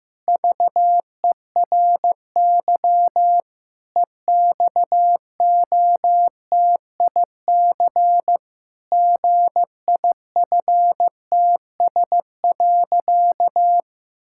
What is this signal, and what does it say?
Morse code, heard as a keyed tone, about 15 wpm, 700 Hz. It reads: VERY EXOTIC GIFTS.
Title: Pangram Challenge v2 - 15wpm